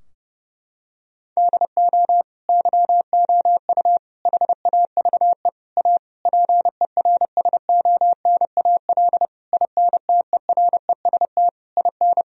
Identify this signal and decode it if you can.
Morse code, heard as a keyed tone, about 30 wpm, 700 Hz. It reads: DO YOU HAVE A PERSONAL INTEREST IN
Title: Sentences from Top 1000 Words - Review Part 4 - 30wpm